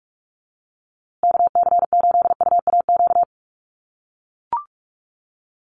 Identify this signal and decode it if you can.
Morse code, heard as a keyed tone, about 45 wpm, 700 Hz. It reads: KC8UAQ
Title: USA Call Signs Repeated 3x 45wpm